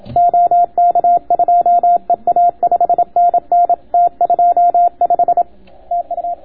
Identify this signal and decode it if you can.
Morse code, heard as a keyed tone, about 27 wpm, 700 Hz. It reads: OK2EA 5NN T25 TU